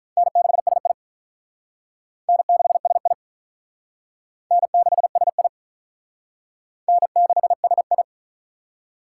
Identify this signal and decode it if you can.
Morse code, heard as a keyed tone, about 44 wpm, 700 Hz. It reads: N6SI N6SI N6SI N6SI
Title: Speed Racing - USA 1x2 Call Signs 35wpm